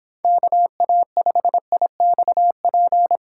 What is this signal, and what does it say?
Morse code, heard as a keyed tone, about 25 wpm, 700 Hz. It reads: KA5IXP